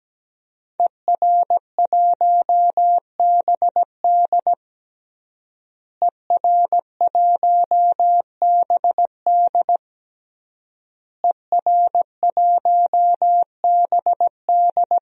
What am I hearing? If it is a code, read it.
Morse code, heard as a keyed tone, about 17 wpm, 700 Hz. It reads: ER1BD ER1BD ER1BD
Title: DXCC-100 Call Signs - 5 Characters - Repeated 3x - 17wpm